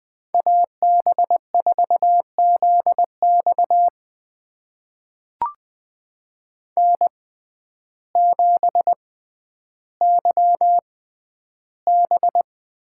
Morse code, heard as a keyed tone, about 20 wpm, spaced as for 11 wpm, 700 Hz. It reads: AB4ZX N7YB
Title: USA Call Signs - 5 Characters - 6x Character Spacing - 20wpm